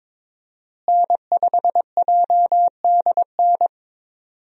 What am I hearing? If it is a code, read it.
Morse code, heard as a keyed tone, about 22 wpm, 700 Hz. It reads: N5JDN
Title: USA Call Signs - 5 Characters - Repeated 3x - 22wpm